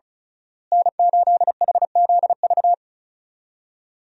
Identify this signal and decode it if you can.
Morse code, heard as a keyed tone, about 36 wpm, 700 Hz. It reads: N8HZV
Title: USA Call Signs - 5 Characters - Repeated 3x - 35wpm